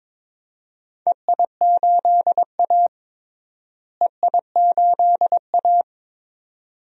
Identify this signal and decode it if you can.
Morse code, heard as a keyed tone, about 22 wpm, 700 Hz. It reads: EI8A EI8A
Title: DXCC-100 Call Signs - 4 Characters - Repeated 2x - 22wpm